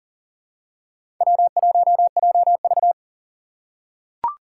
Morse code, heard as a keyed tone, about 37 wpm, 700 Hz. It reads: W1JV
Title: USA Call Signs - 4 Characters - 4x Character Spacing - 40wpm